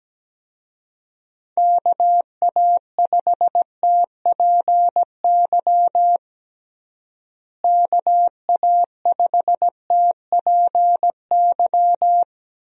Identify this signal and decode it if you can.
Morse code, heard as a keyed tone, about 17 wpm, 700 Hz. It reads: KA5TPY KA5TPY
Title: USA Call Signs - 6 Characters - Repeated 3x - 17wpm